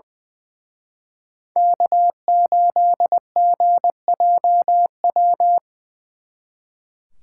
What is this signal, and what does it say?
Morse code, heard as a keyed tone, about 24 wpm, 700 Hz. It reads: K8GJW
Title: Speed Racing - USA 1x3 Call Signs 20wpm